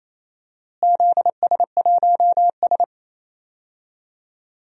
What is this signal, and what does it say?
Morse code, heard as a keyed tone, about 27 wpm, 700 Hz. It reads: ZS1S
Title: DXCC-100 Call Signs - 4 Characters - Repeated 2x - 28wpm